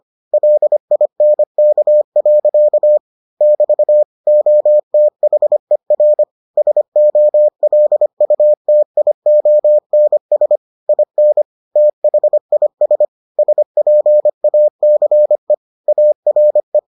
Morse code, heard as a keyed tone, about 25 wpm, 600 Hz. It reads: LINK. <BT> OTHER SOLUTIONS IN THIS SPACE ARE